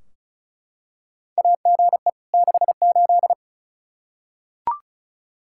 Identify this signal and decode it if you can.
Morse code, heard as a keyed tone, about 36 wpm, 700 Hz. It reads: AGE 68
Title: Practice QSO - Exchange 3 - More Info - 35wpm